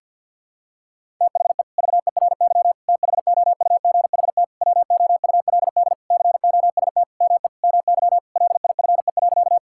Morse code, heard as a keyed tone, about 50 wpm, 700 Hz. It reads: THE VERY THOUGHT WOULD COST ME MY LIFE.
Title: Sentences from Top 1000 Words - Review Part 5 - 50wpm